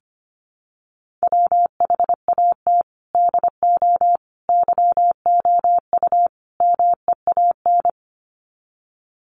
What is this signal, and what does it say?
Morse code, heard as a keyed tone, about 25 wpm, 700 Hz. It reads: WHAT DO YOU MEAN
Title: Sets of 4 Words 25wpm